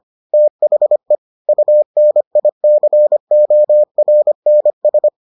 Morse code, heard as a keyed tone, about 25 wpm, 600 Hz. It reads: THE UNICORNS